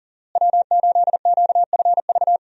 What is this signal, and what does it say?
Morse code, heard as a keyed tone, about 40 wpm, 700 Hz. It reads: W8QFV
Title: USA Call Signs - 5 Characters - Repeated 3x - 40wpm